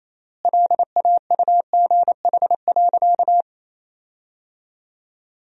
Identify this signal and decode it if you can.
Morse code, heard as a keyed tone, about 28 wpm, 700 Hz. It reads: LAUGH.
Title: 3x Word-Spacing - Sentences from Top 500 Words - 28wpm